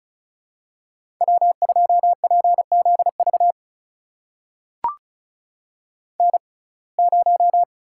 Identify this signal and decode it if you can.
Morse code, heard as a keyed tone, about 36 wpm, spaced as for 25 wpm, 700 Hz. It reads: W2PZV N0
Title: USA Call Signs - 5 Characters - 6x Character Spacing - 35wpm